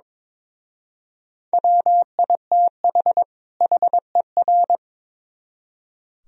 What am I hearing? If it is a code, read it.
Morse code, heard as a keyed tone, about 26 wpm, 700 Hz. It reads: WITH HER
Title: Speed-Racing - Sets of 2 Words 22wpm